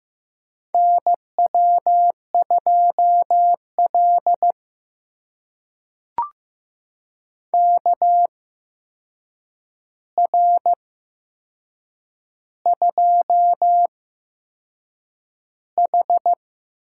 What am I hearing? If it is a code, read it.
Morse code, heard as a keyed tone, about 15 wpm, spaced as for 6 wpm, 700 Hz. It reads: NW2L KR2H
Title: USA Call Signs - 4 Characters - 8x Character Spacing - 15wpm